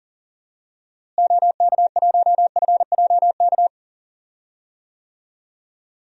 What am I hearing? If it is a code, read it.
Morse code, heard as a keyed tone, about 39 wpm, 700 Hz. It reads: OK1FJK
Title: DXCC-100 Call Signs - 6 Characters - Repeated 3x - 40wpm